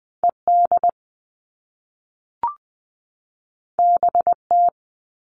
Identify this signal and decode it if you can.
Morse code, heard as a keyed tone, about 20 wpm, 700 Hz. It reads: ED BT
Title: All Cut Numbers - 2-Digit 20wpm